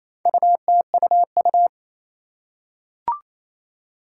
Morse code, heard as a keyed tone, about 27 wpm, 700 Hz. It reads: UTUU
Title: Four Challenging Letter-Number Combinations 28wpm